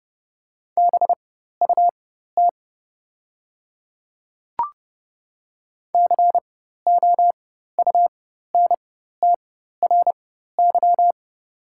Morse code, heard as a keyed tone, about 30 wpm, spaced as for 14 wpm, 700 Hz. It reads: BUT COUNTRY
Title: 4x Character Spacing - Top 300 Words - 30wpm